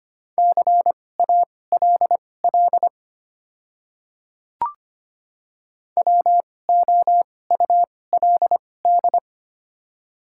Morse code, heard as a keyed tone, about 25 wpm, spaced as for 18 wpm, 700 Hz. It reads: CALL WOULD
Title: 2x Character Spacing - Top 200 Words - 25wpm